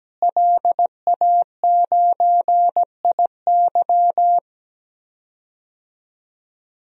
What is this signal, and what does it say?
Morse code, heard as a keyed tone, about 17 wpm, 700 Hz. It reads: LA9IY
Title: DXCC-100 Call Signs - 5 Characters - Repeated 2x - 17wpm